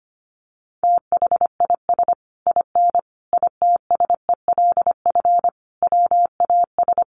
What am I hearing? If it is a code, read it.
Morse code, heard as a keyed tone, about 25 wpm, 700 Hz. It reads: THIS IN ITSELF WAS
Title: Sentences from Top 1400 Words 25wpm